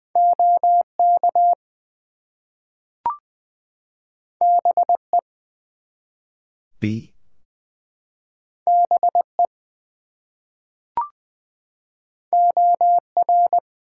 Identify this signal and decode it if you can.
Morse code, heard as a keyed tone, about 20 wpm, 700 Hz. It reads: OK BE BE OR
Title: Two Letter Words - 20wpm